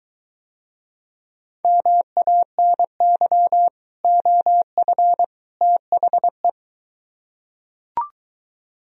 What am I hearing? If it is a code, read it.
Morse code, heard as a keyed tone, about 23 wpm, 700 Hz. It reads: MANY OF THE